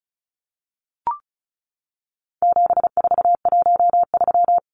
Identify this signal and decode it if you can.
Morse code, heard as a keyed tone, about 34 wpm, 700 Hz. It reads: 7413
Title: Numbers - 4-Digit 35wpm